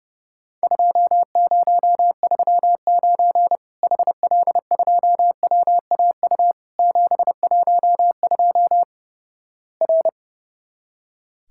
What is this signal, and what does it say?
Morse code, heard as a keyed tone, about 30 wpm, 700 Hz. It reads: 2039 HL2WAU 712 R
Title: Worked All Europe DX Contest - QTC Exchange - 30wpm